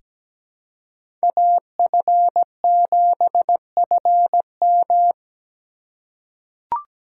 Morse code, heard as a keyed tone, about 17 wpm, 700 Hz. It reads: AF7FM
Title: USA Call Signs - 5 Characters - 2x Character Spacing - 17wpm